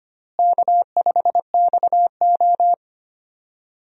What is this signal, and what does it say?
Morse code, heard as a keyed tone, about 25 wpm, 700 Hz. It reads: K5XO
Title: USA Call Signs - 4 Characters - Repeated 3x - 25wpm